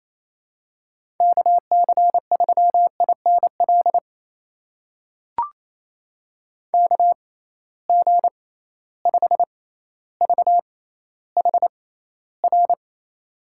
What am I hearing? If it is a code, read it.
Morse code, heard as a keyed tone, about 28 wpm, spaced as for 14 wpm, 700 Hz. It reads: KC3INL KG5VHR
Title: USA Call Signs - 6 Characters - 6x Character Spacing - 28wpm